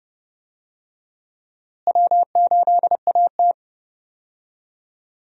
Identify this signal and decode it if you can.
Morse code, heard as a keyed tone, about 32 wpm, 700 Hz. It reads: W8AT
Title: USA Call Signs - 4 Characters - Repeated 2x - 30wpm